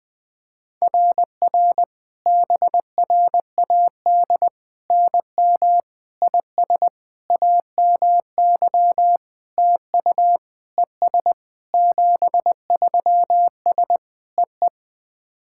Morse code, heard as a keyed tone, about 20 wpm, 700 Hz. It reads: RR BRAD NM IS AMY TU ES 73S EE
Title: SOTA QSO Practice 20wpm